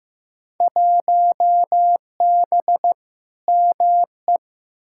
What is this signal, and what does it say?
Morse code, heard as a keyed tone, about 15 wpm, 700 Hz. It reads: 1B ME
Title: Field Day Exchanges - 15wpm